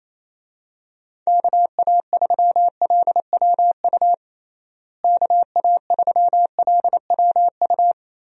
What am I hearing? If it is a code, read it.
Morse code, heard as a keyed tone, about 28 wpm, 700 Hz. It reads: KA3LWU KA3LWU
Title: USA Call Signs - 6 Characters - Repeated 2x - 28wpm